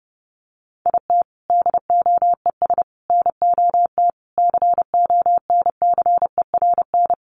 Morse code, heard as a keyed tone, about 28 wpm, 700 Hz. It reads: IT DOES NOT CONCERN
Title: Sentences from Top 1500 Words 30wpm